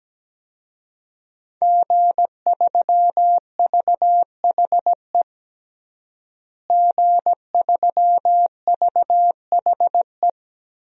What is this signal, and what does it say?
Morse code, heard as a keyed tone, about 17 wpm, 700 Hz. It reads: G3VHE G3VHE
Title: DXCC-100 Call Signs - 5 Characters - Repeated 2x - 17wpm